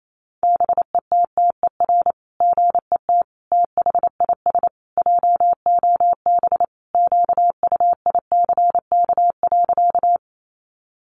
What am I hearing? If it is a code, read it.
Morse code, heard as a keyed tone, about 28 wpm, 700 Hz. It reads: BETTER GET THIS JOB QUICK.
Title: Sentences from Top 600 Words 28wpm